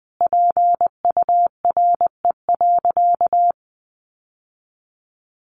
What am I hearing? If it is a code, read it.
Morse code, heard as a keyed tone, about 20 wpm, 700 Hz. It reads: PURE.
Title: Sentences from Top 1500 Words 20wpm